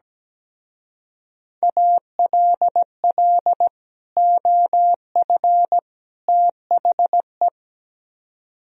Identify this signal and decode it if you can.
Morse code, heard as a keyed tone, about 18 wpm, 700 Hz. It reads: ALL OF THE
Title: Speed-Racing - Sets of 3 Words 17wpm